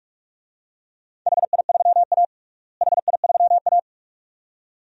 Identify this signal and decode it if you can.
Morse code, heard as a keyed tone, about 41 wpm, 700 Hz. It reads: HI3A HI3A
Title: DXCC-100 Call Signs - 4 Characters - Repeated 2x - 45wpm